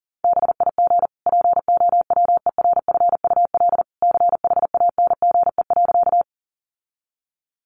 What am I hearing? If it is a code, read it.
Morse code, heard as a keyed tone, about 37 wpm, 700 Hz. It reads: BIG POWERFUL CHANGE.
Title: Sentences from Top 1300 Words 40wpm